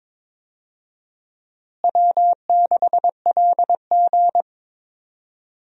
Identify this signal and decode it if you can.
Morse code, heard as a keyed tone, about 22 wpm, 700 Hz. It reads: W6LG
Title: USA Call Signs - 4 Characters - Repeated 3x - 22wpm